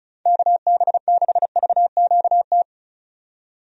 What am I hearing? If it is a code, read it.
Morse code, heard as a keyed tone, about 36 wpm, 700 Hz. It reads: KB6VQT
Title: USA Call Signs - 6 Characters - 35wpm